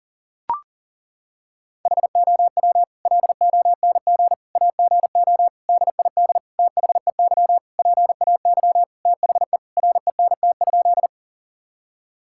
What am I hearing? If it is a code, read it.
Morse code, heard as a keyed tone, about 37 wpm, 700 Hz. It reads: HOW LONG AGO DID THEY PAY THE RENT?
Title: Sentences from Top 1000 Words - Review Part 5 - 40wpm